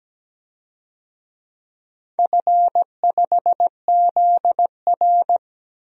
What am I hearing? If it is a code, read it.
Morse code, heard as a keyed tone, about 17 wpm, 700 Hz. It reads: F5ZR
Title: DXCC-100 Call Signs - 4 Characters - 17wpm